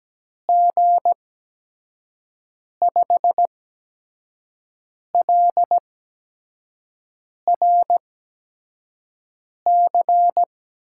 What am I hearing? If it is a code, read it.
Morse code, heard as a keyed tone, about 17 wpm, spaced as for 5 wpm, 700 Hz. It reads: G5LRC
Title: USA Call Signs - 6 Characters - 8x Character Spacing - 17wpm